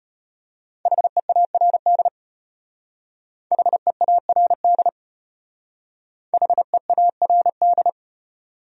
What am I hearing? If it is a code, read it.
Morse code, heard as a keyed tone, about 32 wpm, 700 Hz. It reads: HEARD HEARD HEARD